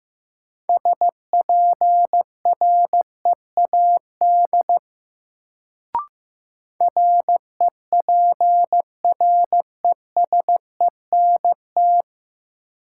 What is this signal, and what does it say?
Morse code, heard as a keyed tone, about 16 wpm, 700 Hz. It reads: SPREAD REPRESENT
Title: Sentences from Top 1200 Words - New Word Drill - 15wpm